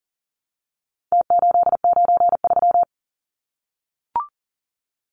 Morse code, heard as a keyed tone, about 40 wpm, 700 Hz. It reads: T893
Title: Numbers - Contest Style - 4-Digit 40wpm